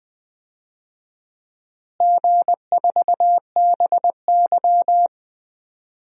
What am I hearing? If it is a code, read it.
Morse code, heard as a keyed tone, about 21 wpm, 700 Hz. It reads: G4BY
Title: DXCC-100 Call Signs - 4 Characters - Repeated 3x - 20wpm